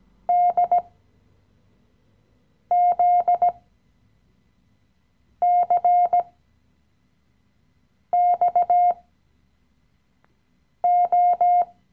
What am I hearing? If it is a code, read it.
Morse code, heard as a keyed tone, about 17 wpm, spaced as for 4 wpm, 710 Hz. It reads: DZCXO
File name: CW-Online-Kurs / Anfänger / Lektion 015